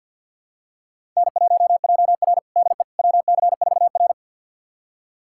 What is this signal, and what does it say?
Morse code, heard as a keyed tone, about 52 wpm, 700 Hz. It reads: N1JR DE WC4R
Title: USA Callsign DE Callsign - 4 Characters - 50wpm